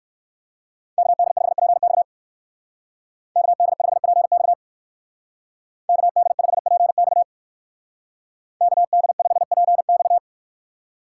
Morse code, heard as a keyed tone, about 59 wpm, 700 Hz. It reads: KD5PX KD5PX KD5PX KD5PX